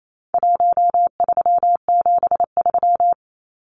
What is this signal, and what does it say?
Morse code, heard as a keyed tone, about 28 wpm, 700 Hz. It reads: 1373
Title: Numbers - Contest Style - 4-Digit 28wpm